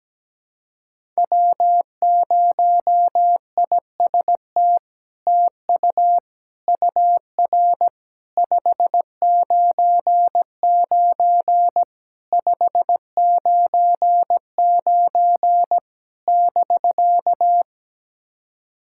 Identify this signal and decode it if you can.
Morse code, heard as a keyed tone, about 17 wpm, 700 Hz. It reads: W0IST TU UR 599 599 <BK>
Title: SOTA QSO Practice 17wpm